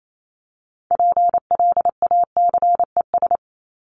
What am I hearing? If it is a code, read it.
Morse code, heard as a keyed tone, about 28 wpm, 700 Hz. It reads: PLACES
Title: ES:S Suffix - Top 100 Words with Base Words 28wpm